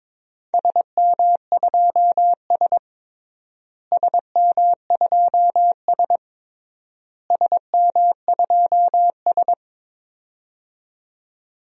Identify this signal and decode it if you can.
Morse code, heard as a keyed tone, about 22 wpm, 700 Hz. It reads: SM2S SM2S SM2S
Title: DXCC-100 Call Signs - 4 Characters - Repeated 3x - 22wpm